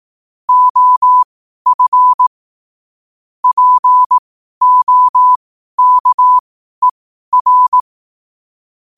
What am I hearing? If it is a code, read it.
Morse code, heard as a keyed tone, about 18 wpm, spaced as for 12 wpm, 1000 Hz. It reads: OF POKER
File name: qotdfiles / 12wpm